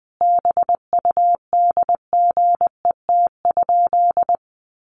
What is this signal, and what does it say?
Morse code, heard as a keyed tone, about 20 wpm, 700 Hz. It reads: BUDGET?
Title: Sentences from Top 1200 Words 20wpm